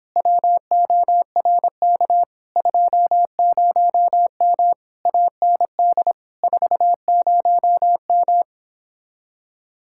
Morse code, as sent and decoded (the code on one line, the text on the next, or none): .-- --- .-. -.- / ..--- ----- -- / .- -. -.. / ....- ----- --
WORK 20M AND 40M